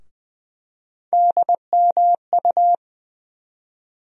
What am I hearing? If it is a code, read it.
Morse code, heard as a keyed tone, about 20 wpm, 700 Hz. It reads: DMU
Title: USA Call Sign Postfixes - 3 Letters 20wpm